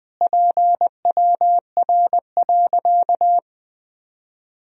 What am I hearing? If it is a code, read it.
Morse code, heard as a keyed tone, about 20 wpm, 700 Hz. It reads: PWR.
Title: Sentences from Top 500 Words and Abbreviations 20wpm